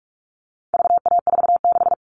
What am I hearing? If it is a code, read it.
Morse code, heard as a keyed tone, about 44 wpm, 700 Hz. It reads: VA46
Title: All Cut Numbers - 4-Digit 45wpm